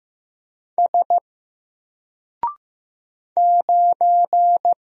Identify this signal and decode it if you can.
Morse code, heard as a keyed tone, about 15 wpm, 700 Hz. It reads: S 9
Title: Single Character (Letter:Number:Punctuation:Prosign) 15wpm